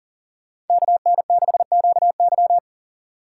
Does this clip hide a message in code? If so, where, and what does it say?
Yes: Morse code from 0.7 to 2.6 s, heard as a keyed tone, about 52 wpm, 700 Hz. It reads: KN6QY